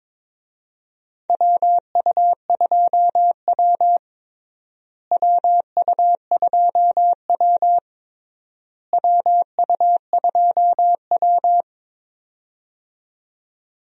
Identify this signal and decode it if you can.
Morse code, heard as a keyed tone, about 21 wpm, 700 Hz. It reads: WU2W WU2W WU2W
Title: USA Call Signs - 4 Characters - Repeated 3x - 22wpm